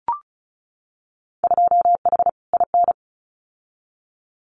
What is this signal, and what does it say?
Morse code, heard as a keyed tone, about 36 wpm, 700 Hz. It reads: 2H IN
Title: Winter Field Day Exchanges Encore 35wpm